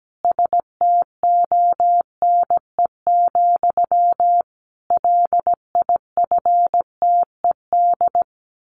Morse code, heard as a keyed tone, about 17 wpm, 700 Hz. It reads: STONE, LIFTED
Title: A Sound Of Thunder 17wpm